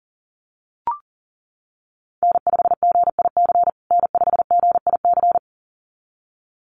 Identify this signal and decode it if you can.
Morse code, heard as a keyed tone, about 37 wpm, 700 Hz. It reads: N5GIC N5GIC
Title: USA Call Signs Repeated 2x 40wpm